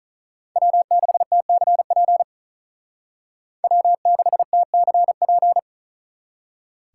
Morse code, heard as a keyed tone, about 45 wpm, 700 Hz. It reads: W6TCP W6TCP